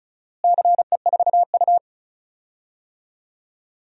Morse code, heard as a keyed tone, about 36 wpm, 700 Hz. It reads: CE4U
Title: DXCC-100 Call Signs - 4 Characters - Repeated 2x - 35wpm